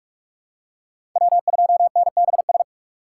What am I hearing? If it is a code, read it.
Morse code, heard as a keyed tone, about 44 wpm, 700 Hz. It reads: W2NBS
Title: USA Call Signs - 5 Characters - 3x Character Spacing - 45wpm